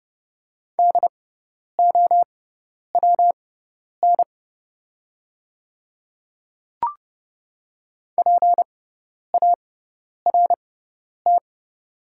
Morse code, heard as a keyed tone, about 31 wpm, spaced as for 10 wpm, 690 Hz. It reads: DOWN PART
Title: 6x Character Spacing - Top 200 Words - 30wpm